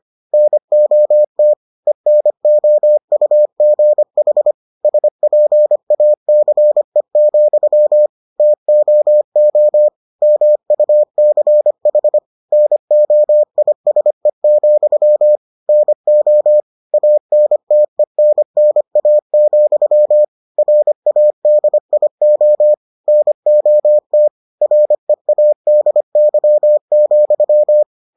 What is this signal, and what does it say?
Morse code, heard as a keyed tone, about 25 wpm, 600 Hz. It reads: NOT ENOUGH SPACE, TOO MUCH NOISE, NO ANTENNA, RADIO NOT READY,